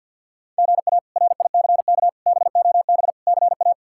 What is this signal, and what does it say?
Morse code, heard as a keyed tone, about 46 wpm, 700 Hz. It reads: GA RICK BOB CA
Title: K1USN SST QSO Practice - 50wpm